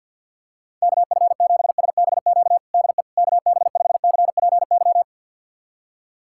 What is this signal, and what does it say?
Morse code, heard as a keyed tone, about 51 wpm, 700 Hz. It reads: KF7SBQ DE KB5CPY
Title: USA Callsign DE Callsign 50wpm